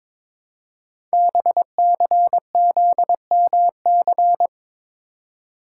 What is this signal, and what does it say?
Morse code, heard as a keyed tone, about 23 wpm, 700 Hz. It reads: BCZMC
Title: Code Groups - 5 Letters - 22wpm